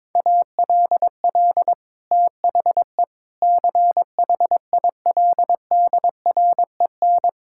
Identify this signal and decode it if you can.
Morse code, heard as a keyed tone, about 21 wpm, 700 Hz. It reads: ALL THE CHILDREN
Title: Sentences from Top 1000 Words - Review Part 6 - 22wpm